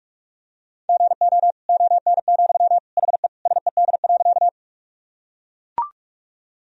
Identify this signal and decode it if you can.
Morse code, heard as a keyed tone, about 42 wpm, 700 Hz. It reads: GO ON, HE SED.
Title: Sentences from Top 100 Words and Abbreviations 45wpm